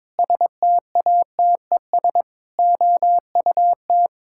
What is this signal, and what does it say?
Morse code, heard as a keyed tone, about 22 wpm, 700 Hz. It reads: STATES OUT
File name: practice / 22wpm / Sentences with 8-Word Phrases - 22wpm